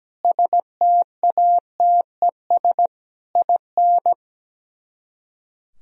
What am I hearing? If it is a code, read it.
Morse code, heard as a keyed tone, about 17 wpm, 700 Hz. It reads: STATES IN
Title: Speed-Racing - Sets of 5 Words 17wpm